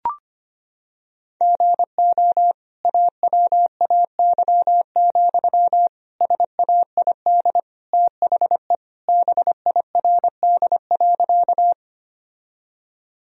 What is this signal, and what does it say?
Morse code, heard as a keyed tone, about 25 wpm, 700 Hz. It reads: GO AWAY, SAID THE BIRD.
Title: Sentences from Top 1000 Words - Review Part 2 - 25wpm